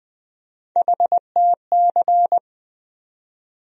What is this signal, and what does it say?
Morse code, heard as a keyed tone, about 20 wpm, 700 Hz. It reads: HTC
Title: USA Call Sign Postfixes - 3 Letters 20wpm